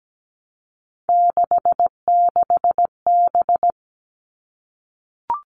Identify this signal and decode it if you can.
Morse code, heard as a keyed tone, about 17 wpm, 700 Hz. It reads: 66B
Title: Three Challenging Letter-Number Combinations 17wpm